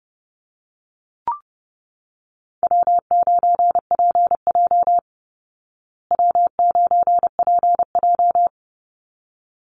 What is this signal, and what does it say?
Morse code, heard as a keyed tone, about 30 wpm, 700 Hz. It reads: W9PJ W9PJ
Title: Easy Call Signs Repeated 2x - 30wpm